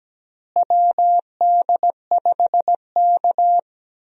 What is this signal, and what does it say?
Morse code, heard as a keyed tone, about 17 wpm, 700 Hz. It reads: WD5K
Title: USA Call Signs - 4 Characters - Repeated 3x - 17wpm